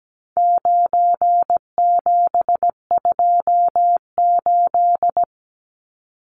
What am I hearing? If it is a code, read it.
Morse code, heard as a keyed tone, about 17 wpm, 700 Hz. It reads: 9728
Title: Numbers - 4-Digit 17wpm